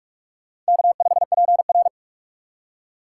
Morse code, heard as a keyed tone, about 56 wpm, 700 Hz. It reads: K5PR
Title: Speed Racing - USA 1x2 Call Signs 45wpm